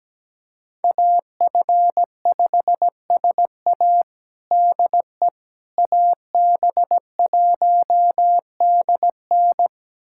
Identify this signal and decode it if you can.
Morse code, heard as a keyed tone, about 17 wpm, 700 Hz. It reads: AF5SA DE AB1DN
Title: USA Callsign DE Callsign - 5 Characters - 17wpm